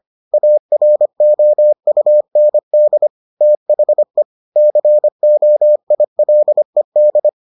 Morse code, heard as a keyed tone, about 25 wpm, 600 Hz. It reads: AROUND THE COILED